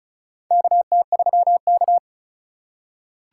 Morse code, heard as a keyed tone, about 36 wpm, 700 Hz. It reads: KT3K
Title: USA Call Signs - 4 Characters - 4x Character Spacing - 35wpm